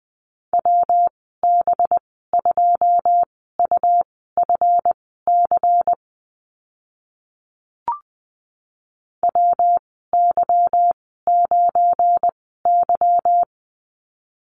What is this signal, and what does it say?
Morse code, heard as a keyed tone, about 20 wpm, spaced as for 15 wpm, 700 Hz. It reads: WB2UFC WY9Y
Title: 186 Lesson 30g - Letter b - Call Signs 20wpm